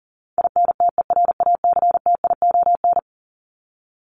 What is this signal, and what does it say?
Morse code, heard as a keyed tone, about 40 wpm, 700 Hz. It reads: INTERACTION
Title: Top 1251-1500 Words 40wpm